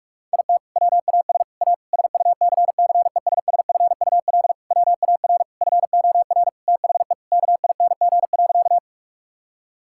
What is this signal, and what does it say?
Morse code, heard as a keyed tone, about 46 wpm, 700 Hz. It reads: IT WAS A SUCCESSFUL WAR FOR THE KING.
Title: Sentences from Top 1000 Words - Review Part 6 - 45wpm